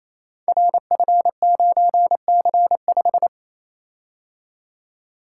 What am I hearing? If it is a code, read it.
Morse code, heard as a keyed tone, about 28 wpm, 700 Hz. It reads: RF9C5
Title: Code Groups - 5 Letter:Numbers - 28wpm